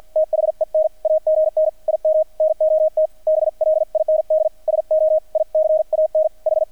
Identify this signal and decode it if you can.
Morse code, heard as a keyed tone, about 50 wpm, 630 Hz, with radio noise. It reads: THEN AGN IM NOT BLIND SO I CAN H
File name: archived / CRSnet